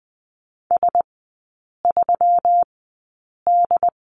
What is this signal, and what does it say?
Morse code, heard as a keyed tone, about 20 wpm, spaced as for 8 wpm, 700 Hz. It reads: S3D
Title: 138 Lesson 23e - Letter f - Three Characters 20wpm